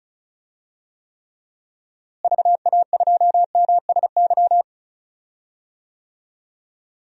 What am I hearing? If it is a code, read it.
Morse code, heard as a keyed tone, about 34 wpm, 700 Hz. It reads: VA2MSY
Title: DXCC-100 Call Signs - 6 Characters - Repeated 3x - 35wpm